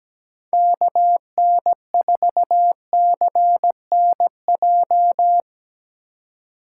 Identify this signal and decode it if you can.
Morse code, heard as a keyed tone, about 17 wpm, 700 Hz. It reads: KN4CNJ
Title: USA Call Signs - 6 Characters - 3x Character Spacing - 17wpm